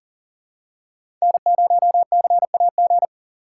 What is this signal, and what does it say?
Morse code, heard as a keyed tone, about 40 wpm, 700 Hz. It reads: N0CAG
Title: USA Call Signs - 3x Character Spacing - 40wpm